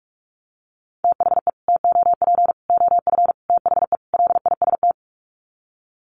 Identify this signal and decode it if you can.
Morse code, heard as a keyed tone, about 44 wpm, 700 Hz. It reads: THE TOP OF THE LIST
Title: Sets of 5 Words Encore2 45wpm